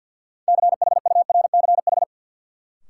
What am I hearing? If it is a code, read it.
Morse code, heard as a keyed tone, about 53 wpm, 700 Hz. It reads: CHURCH
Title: Sets of 5 Words - New Word Drill - 50wpm